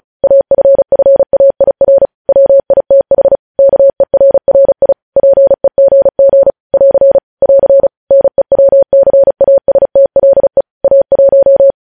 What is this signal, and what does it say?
Morse code, heard as a keyed tone, about 36 wpm, 550 Hz. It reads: AFFAIR WITH KERRI PEGG <AR> <AR> NEWCASTLE A1
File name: podcast / News headlines-1x-35x35